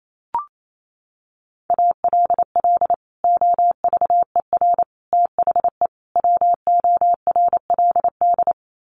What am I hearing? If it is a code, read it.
Morse code, heard as a keyed tone, about 28 wpm, 700 Hz. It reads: ALL OVER THE WORLD
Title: Sets of 4 Words 28wpm